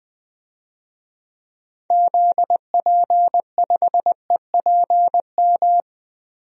Morse code, heard as a keyed tone, about 20 wpm, 700 Hz. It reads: ZP5EPM
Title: DXCC-100 Call Signs - 6 Characters - 20wpm